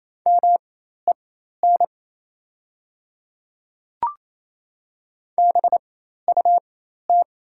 Morse code, heard as a keyed tone, about 27 wpm, spaced as for 13 wpm, 700 Hz. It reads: MEN BUT
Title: 4x Character Spacing - Top 200 Words - 28wpm